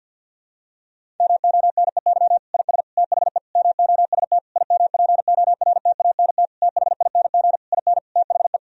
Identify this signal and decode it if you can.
Morse code, heard as a keyed tone, about 52 wpm, 700 Hz. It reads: MONEY IS THE MOST IMPORTANT THING IN THE
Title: Sentences from Top 1000 Words - Review Part 1 - 50wpm